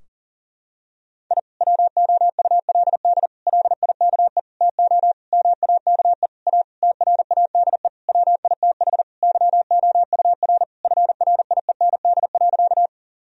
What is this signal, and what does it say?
Morse code, heard as a keyed tone, about 40 wpm, 700 Hz. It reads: I WOULD LIKE TO MAKE A TRADE WITH YOUR FRIEND.
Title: Sentences from Top 1000 Words - Review Part 4 - 40wpm